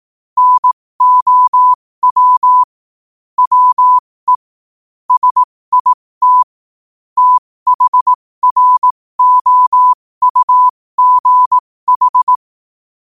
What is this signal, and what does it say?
Morse code, heard as a keyed tone, about 18 wpm, spaced as for 15 wpm, 1000 Hz. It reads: NOW WE SIT THROUGH
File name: qotdfiles / 15wpm